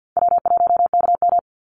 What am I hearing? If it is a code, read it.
Morse code, heard as a keyed tone, about 50 wpm, 700 Hz. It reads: W1KM
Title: Easy Call Signs Repeated 2x - 50wpm